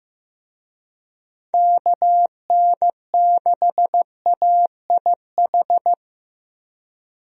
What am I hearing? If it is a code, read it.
Morse code, heard as a keyed tone, about 15 wpm, 700 Hz. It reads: KN6AIH
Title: USA Call Signs - 6 Characters - Repeated 2x - 15wpm